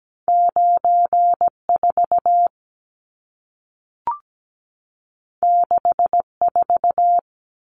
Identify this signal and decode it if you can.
Morse code, heard as a keyed tone, about 17 wpm, 700 Hz. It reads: 94 64
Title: Numbers - 2-Digit 17wpm